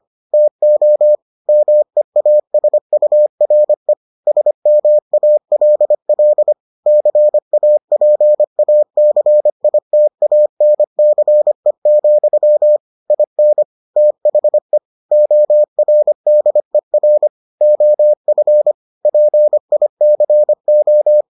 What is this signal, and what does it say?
Morse code, heard as a keyed tone, about 25 wpm, 600 Hz. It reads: TO MEASURE SMALL CAPACITANCE, IN THE ORDER OF PICO